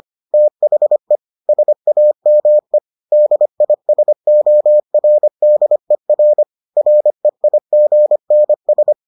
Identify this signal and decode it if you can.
Morse code, heard as a keyed tone, about 25 wpm, 600 Hz. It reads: THE SAME DISORDER REIGNS